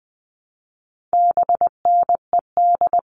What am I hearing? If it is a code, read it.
Morse code, heard as a keyed tone, about 20 wpm, 700 Hz. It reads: BNED
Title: All Cut Numbers - 4-Digit 20wpm